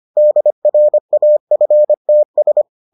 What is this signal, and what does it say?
Morse code, heard as a keyed tone, about 25 wpm, 600 Hz. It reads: DRAFTS